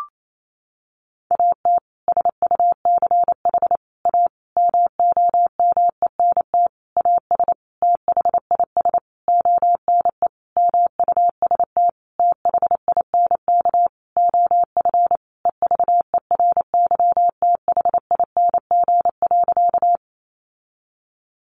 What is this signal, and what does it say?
Morse code, heard as a keyed tone, about 28 wpm, 700 Hz. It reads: AT SUCH A MOMENT AS THIS ONE MUST THINK OF EVERYTHING.
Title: Sentences from Top 1000 Words 28wpm